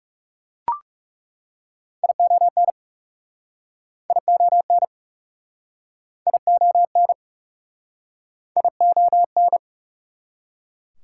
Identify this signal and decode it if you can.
Morse code, heard as a keyed tone, about 37 wpm, 690 Hz. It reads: ION ION ION ION
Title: Speed-Racing Prefixes and Suffixes 30wpm